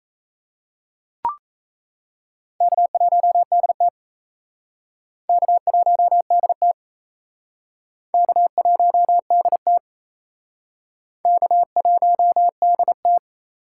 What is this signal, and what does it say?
Morse code, heard as a keyed tone, about 36 wpm, 700 Hz. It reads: K1DT K1DT K1DT K1DT